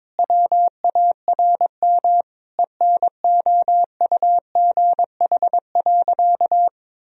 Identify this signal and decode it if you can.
Morse code, heard as a keyed tone, about 22 wpm, 700 Hz. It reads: WARM ENOUGH.
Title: Sentences from Top 1000 Words - Review Part 2 - 22wpm